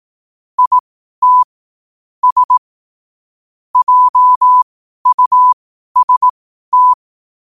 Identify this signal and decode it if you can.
Morse code, heard as a keyed tone, about 18 wpm, spaced as for 12 wpm, 1000 Hz. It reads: IT S JUST
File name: qotdfiles / 12wpm